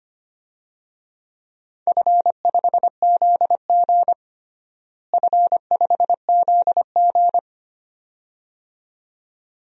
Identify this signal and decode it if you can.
Morse code, heard as a keyed tone, about 25 wpm, 700 Hz. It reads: F5ZG F5ZG
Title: DXCC-100 Call Signs - 4 Characters - Repeated 2x - 25wpm